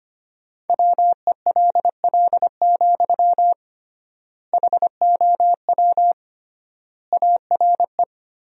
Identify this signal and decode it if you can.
Morse code, heard as a keyed tone, about 25 wpm, 700 Hz. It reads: WELL, HOW ARE
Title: 3x Word-Spacing - Sentences from Top 200 Words - 25wpm